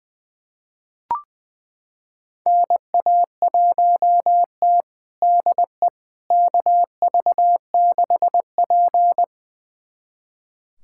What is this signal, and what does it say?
Morse code, heard as a keyed tone, about 20 wpm, 700 Hz. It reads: NA1T DE KV6P
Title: USA Callsign DE Callsign - 4 Characters - 20wpm